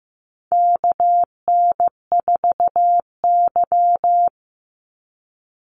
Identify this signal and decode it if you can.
Morse code, heard as a keyed tone, about 15 wpm, 700 Hz. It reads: KN4Y
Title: Call Signs Repeated 2x - 15wpm